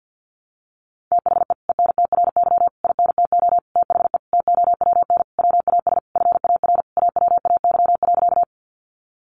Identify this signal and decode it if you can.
Morse code, heard as a keyed tone, about 50 wpm, 700 Hz. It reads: THE ENTRY INTO THE TOWN WAS FAR AWAY.
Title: Sentences from Top 1500 Words 50wpm